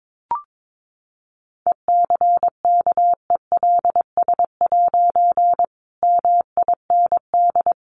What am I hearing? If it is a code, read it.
Morse code, heard as a keyed tone, about 22 wpm, 700 Hz. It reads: ECKELS' MIND
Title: A Sound Of Thunder 22wpm